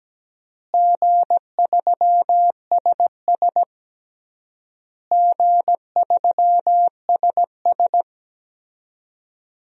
Morse code, heard as a keyed tone, about 17 wpm, 700 Hz. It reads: G3SS G3SS
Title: DXCC-100 Call Signs - 4 Characters - Repeated 3x - 17wpm